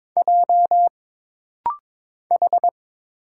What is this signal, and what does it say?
Morse code, heard as a keyed tone, about 22 wpm, 700 Hz. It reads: J H
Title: Single Letter-Number Alt - Rapid-Fire 22wpm